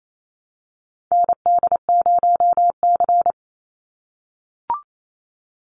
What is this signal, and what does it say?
Morse code, heard as a keyed tone, about 28 wpm, 700 Hz. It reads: ND0C
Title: NA SOTA Chaser Callsigns 28wpm